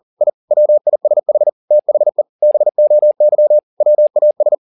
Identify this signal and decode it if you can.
Morse code, heard as a keyed tone, about 40 wpm, 600 Hz. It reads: I WISH THE BOY WAS